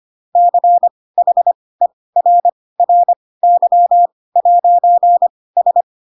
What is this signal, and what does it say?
Morse code, heard as a keyed tone, about 25 wpm, spaced as for 18 wpm, 700 Hz. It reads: CHERRY'S
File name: Pinocchio / Pinocchio ch1 12 25 ik1hge